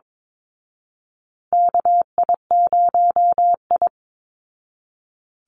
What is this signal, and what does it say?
Morse code, heard as a keyed tone, about 22 wpm, 700 Hz. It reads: KI0I
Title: Call Signs Repeated 2x - 22wpm